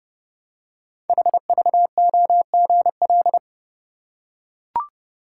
Morse code, heard as a keyed tone, about 30 wpm, 700 Hz. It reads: HVOGL
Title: Code Groups - 5 Letter:Numbers - 30wpm